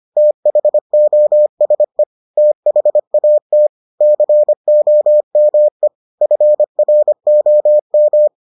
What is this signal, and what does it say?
Morse code, heard as a keyed tone, about 25 wpm, 600 Hz. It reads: THOSE THAT COME FROM